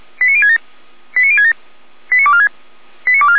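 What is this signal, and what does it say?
Five-tone selective call (ZVEI), heard as quick runs of five tones, tones of about 70 ms. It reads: 89067 89067 89246 89246
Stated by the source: ZVEI five-tone selective call received by SDR.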